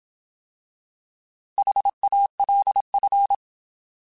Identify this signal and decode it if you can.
Morse code, heard as a keyed tone, about 27 wpm, 800 Hz. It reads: HALF
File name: Alphabetical / Alphabetical mp3s / Hs @25